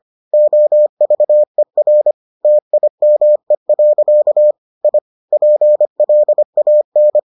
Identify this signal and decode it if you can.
Morse code, heard as a keyed tone, about 25 wpm, 600 Hz. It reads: OVER TIME. I PLAN